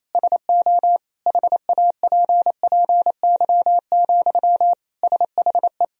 Morse code, heard as a keyed tone, about 28 wpm, 700 Hz. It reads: SO HAPPY, SHE
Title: Sentences from Top 1000 Words - Review Part 1 - 28wpm